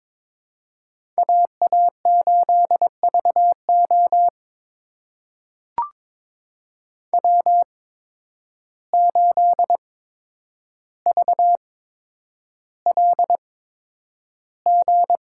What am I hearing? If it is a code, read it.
Morse code, heard as a keyed tone, about 22 wpm, spaced as for 9 wpm, 700 Hz. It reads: AA8VO W8VLG
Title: USA Call Signs - 5 Characters - 8x Character Spacing - 22wpm